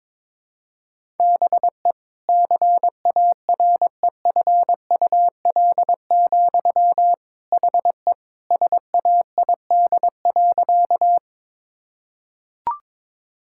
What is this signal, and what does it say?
Morse code, heard as a keyed tone, about 22 wpm, 700 Hz. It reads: BE CAREFUL, HE SAID.
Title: Sentences from Top 1600 Words 22wpm